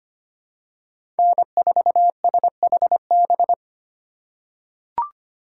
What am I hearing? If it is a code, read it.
Morse code, heard as a keyed tone, about 25 wpm, 700 Hz. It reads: N4SHB
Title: USA Call Signs - 3x Character Spacing - 25wpm